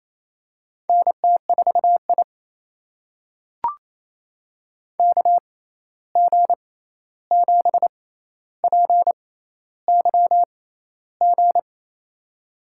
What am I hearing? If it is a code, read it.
Morse code, heard as a keyed tone, about 28 wpm, spaced as for 13 wpm, 700 Hz. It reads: NT4I KG7PYG
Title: USA Call Signs - 6x Character Spacing - 28wpm